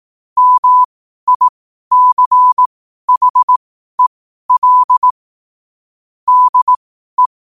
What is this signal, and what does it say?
Morse code, heard as a keyed tone, about 18 wpm, spaced as for 12 wpm, 1000 Hz. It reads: MICHEL DE
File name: qotdfiles / 12wpm